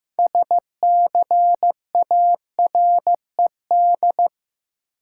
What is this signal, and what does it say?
Morse code, heard as a keyed tone, about 15 wpm, 700 Hz. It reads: SCARED
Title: Sentences from Top 700 Words - New Word Drill - 15wpm